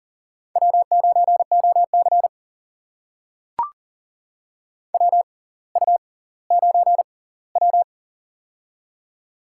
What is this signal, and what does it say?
Morse code, heard as a keyed tone, about 40 wpm, spaced as for 20 wpm, 700 Hz. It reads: W9OC WU9W
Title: USA Call Signs - 4 Characters - 6x Character Spacing - 40wpm